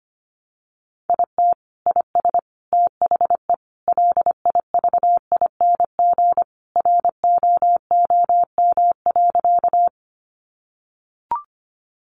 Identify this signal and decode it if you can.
Morse code, heard as a keyed tone, about 25 wpm, 700 Hz. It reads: IT IS THE LIVING ROOM.
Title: Sentences from Top 1100 Words 25wpm